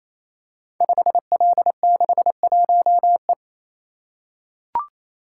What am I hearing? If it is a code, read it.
Morse code, heard as a keyed tone, about 27 wpm, 700 Hz. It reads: 5L61E
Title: Code Groups - 5 Letter:Numbers - 28wpm